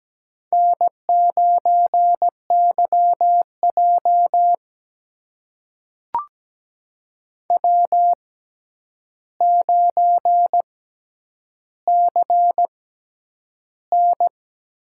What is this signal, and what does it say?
Morse code, heard as a keyed tone, about 17 wpm, spaced as for 9 wpm, 700 Hz. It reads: N9YJ W9CN
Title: USA Call Signs - 6x Character Spacing - 17wpm